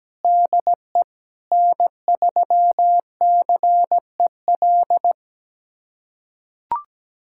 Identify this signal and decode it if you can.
Morse code, heard as a keyed tone, about 17 wpm, 700 Hz. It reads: DE N3CEL
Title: USA Callsign DE Callsign 17wpm